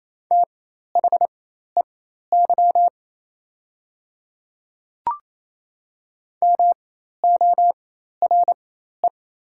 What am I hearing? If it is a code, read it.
Morse code, heard as a keyed tone, about 28 wpm, spaced as for 13 wpm, 700 Hz. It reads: THEY MORE
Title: 4x Character Spacing - Top 100 Words - 28wpm